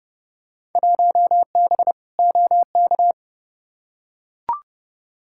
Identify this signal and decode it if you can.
Morse code, heard as a keyed tone, about 32 wpm, 700 Hz. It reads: 1B OK
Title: Field Day Exchanges Encore - 30wpm